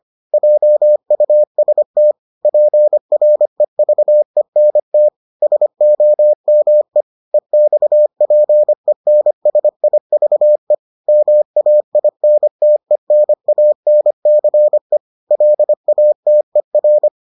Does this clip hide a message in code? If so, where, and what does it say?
Yes: Morse code from 0.3 to 17.1 s, heard as a keyed tone, about 25 wpm, 600 Hz. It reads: JUST PREVENT SOME EXPENSIVE MAINTENANCE LATER